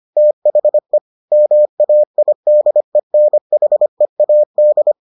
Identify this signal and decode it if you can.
Morse code, heard as a keyed tone, about 25 wpm, 600 Hz. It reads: THE MAIDENHEAD